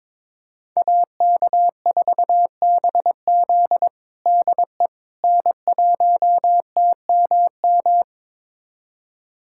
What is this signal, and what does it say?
Morse code, heard as a keyed tone, about 22 wpm, 700 Hz. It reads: AK4BZ DE N1TMM
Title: USA Callsign DE Callsign - 5 Characters - 22wpm